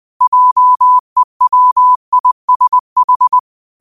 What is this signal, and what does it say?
Morse code, heard as a keyed tone, about 20 wpm, 1000 Hz. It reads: JEWISH
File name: qotdfiles / 20wpm